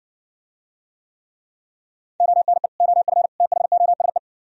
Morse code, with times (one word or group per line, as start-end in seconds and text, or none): 2.2-2.7 ONE
2.8-3.3 OF
3.4-4.2 THOSE